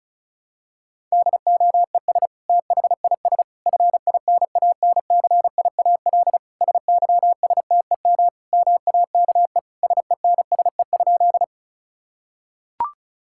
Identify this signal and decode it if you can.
Morse code, heard as a keyed tone, about 36 wpm, 700 Hz. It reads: DOES THIS FINANCIAL SYSTEM MAKE SENSE?
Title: Sentences from Top 1000 Words - Review Part 5 - 35wpm